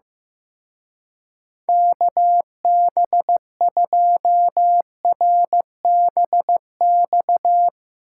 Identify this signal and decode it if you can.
Morse code, heard as a keyed tone, about 15 wpm, 700 Hz. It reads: KB2RBX
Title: Speed Racing - USA 2x3 Call Signs 15wpm